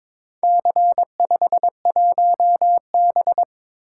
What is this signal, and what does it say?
Morse code, heard as a keyed tone, about 22 wpm, 700 Hz. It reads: C51B
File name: practice / 22wpm / Four Letter-Number Combinations - 22wpm